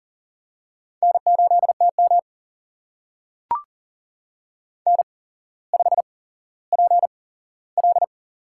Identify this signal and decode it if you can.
Morse code, heard as a keyed tone, about 40 wpm, spaced as for 17 wpm, 700 Hz. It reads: N8TM N5PL
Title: USA Call Signs - 8x Character Spacing - 40wpm